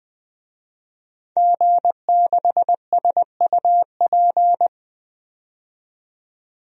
Morse code, heard as a keyed tone, about 21 wpm, 700 Hz. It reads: G6SUP
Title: DXCC-100 Call Signs - 5 Characters - 20wpm